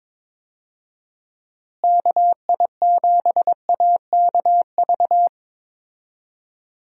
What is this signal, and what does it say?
Morse code, heard as a keyed tone, about 22 wpm, 700 Hz. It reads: KI7AKV
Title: USA Call Signs - 6 Characters - Repeated 2x - 22wpm